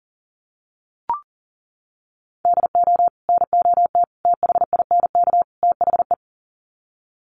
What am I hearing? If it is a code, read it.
Morse code, heard as a keyed tone, about 40 wpm, 700 Hz. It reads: DO NOT THINK THE
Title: Sets of 4 Words 40wpm